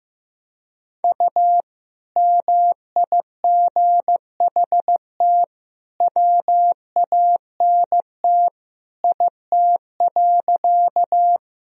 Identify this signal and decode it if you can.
Morse code, heard as a keyed tone, about 15 wpm, 700 Hz. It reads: U MIGHT WANT IT.
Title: Sentences from Top 300 Words and Abbreviations 15wpm